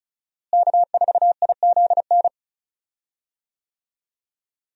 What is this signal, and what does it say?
Morse code, heard as a keyed tone, about 36 wpm, 700 Hz. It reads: K4IZN
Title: USA Call Signs - 5 Characters - Repeated 2x - 35wpm